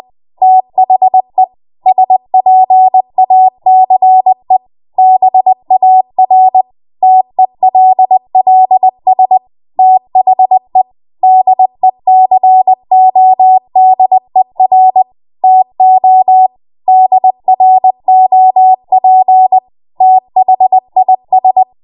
Morse code, heard as a keyed tone, about 20 wpm, 750 Hz. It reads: THE SPACE BAR TELLS THE DECODER TO DROP THIS
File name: Morse / 20WPM